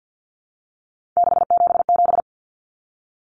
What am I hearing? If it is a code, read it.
Morse code, heard as a keyed tone, about 50 wpm, 700 Hz. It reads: 677
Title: Numbers - 3-Digit 50wpm